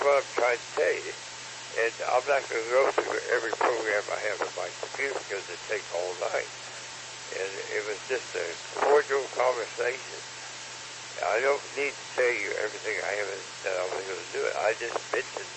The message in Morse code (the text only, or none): none